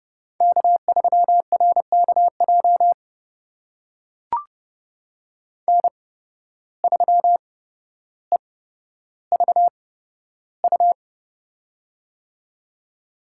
Morse code, heard as a keyed tone, about 32 wpm, spaced as for 13 wpm, 700 Hz. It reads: K3RKJ N3EVU
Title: USA Call Signs - 5 Characters - 8x Character Spacing - 30wpm